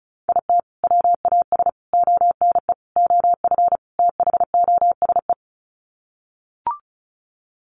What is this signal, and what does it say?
Morse code, heard as a keyed tone, about 34 wpm, 700 Hz. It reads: IT WAS ONE OF THOSE
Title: Sets of 5 Words Encore 35wpm